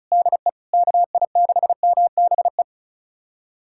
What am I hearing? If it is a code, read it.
Morse code, heard as a keyed tone, about 34 wpm, 700 Hz. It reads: DE KI6MBE
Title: USA Callsign DE Callsign - 6 Characters - 35wpm